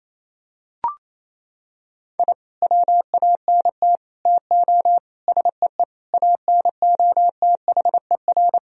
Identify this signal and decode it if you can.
Morse code, heard as a keyed tone, about 28 wpm, 700 Hz. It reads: I WANT TO SEE ANOTHER
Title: Sentences from Top 1000 Words - Review Part 1 - 28wpm